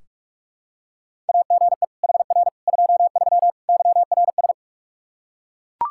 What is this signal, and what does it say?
Morse code, heard as a keyed tone, about 41 wpm, 700 Hz. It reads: AGE HR 23 YRS
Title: Practice QSO - Exchange 3 - More Info - 45wpm